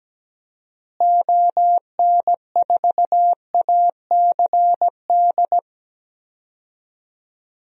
Morse code, heard as a keyed tone, about 17 wpm, 700 Hz. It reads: ON4ACD
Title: DXCC-100 Call Signs - 6 Characters - Repeated 2x - 17wpm